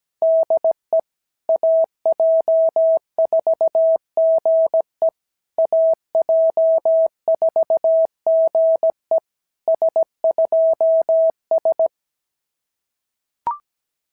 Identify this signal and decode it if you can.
Morse code, heard as a keyed tone, about 17 wpm, 690 Hz. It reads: DE AJ4GE AJ4GE S2S
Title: SOTA QSO Practice 17wpm